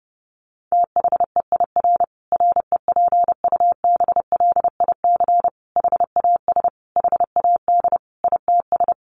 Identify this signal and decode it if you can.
Morse code, heard as a keyed tone, about 30 wpm, 700 Hz. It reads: THEIR REPUBLIC HAS HAD ITS
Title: Sentences from Top 1500 Words 30wpm